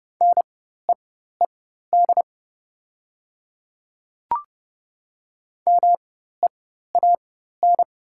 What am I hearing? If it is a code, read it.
Morse code, heard as a keyed tone, about 31 wpm, spaced as for 14 wpm, 700 Hz. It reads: NEED MEAN
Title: 4x Character Spacing - Top 200 Words - 30wpm